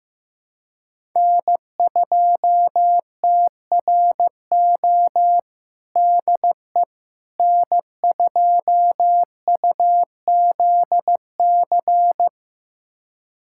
Morse code, heard as a keyed tone, about 15 wpm, 700 Hz. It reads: N2TRO DE N2UZC
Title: USA Callsign DE Callsign - 5 Characters - 15wpm